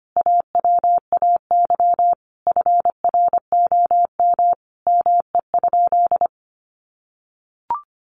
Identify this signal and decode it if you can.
Morse code, heard as a keyed tone, about 25 wpm, 700 Hz. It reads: AWAY FROM ME?
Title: Sentences from Top 1500 Words 25wpm